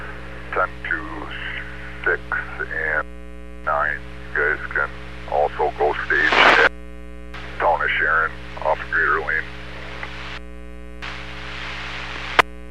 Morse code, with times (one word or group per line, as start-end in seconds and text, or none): none